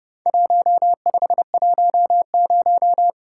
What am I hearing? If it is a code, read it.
Morse code, heard as a keyed tone, about 30 wpm, 700 Hz. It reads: 1510
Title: Worked All Europe DX Contest - QTC Exchange - 30wpm